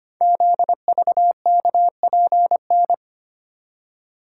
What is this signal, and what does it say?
Morse code, heard as a keyed tone, about 25 wpm, 700 Hz. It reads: ZVKPN
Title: Code Groups - 5 Letters - 25wpm